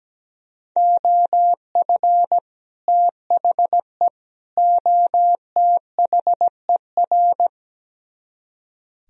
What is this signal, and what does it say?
Morse code, heard as a keyed tone, about 17 wpm, 700 Hz. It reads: OF THE OTHER
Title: Speed-Racing - Sets of 3 Words 17wpm